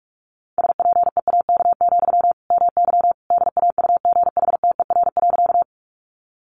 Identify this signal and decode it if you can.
Morse code, heard as a keyed tone, about 44 wpm, 700 Hz. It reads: SPEAK, MY DAUGHTER.
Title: Sentences from Top 1500 Words 45wpm